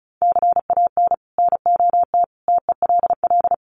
Morse code, heard as a keyed tone, about 34 wpm, 700 Hz. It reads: CAN NOT TELL
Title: Sets of 5 Words Encore2 35wpm